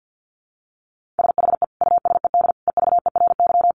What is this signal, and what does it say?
Morse code, heard as a keyed tone, about 50 wpm, 700 Hz. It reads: SHE USED EVERY
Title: Sentences from Top 1400 Words 50wpm